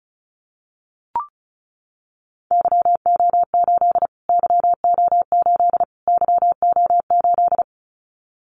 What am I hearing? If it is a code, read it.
Morse code, heard as a keyed tone, about 34 wpm, 700 Hz. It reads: YO8 YO8 YO8
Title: Callsign Prefixes for Top 25 DXCC - Repeated 3x - 35wpm